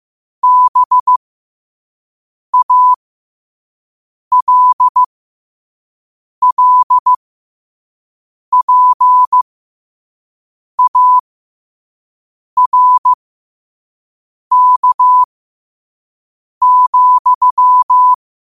Morse code, heard as a keyed tone, about 15 wpm, spaced as for 5 wpm, 1000 Hz. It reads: BALLPARK,
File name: qotdfiles / 5wpm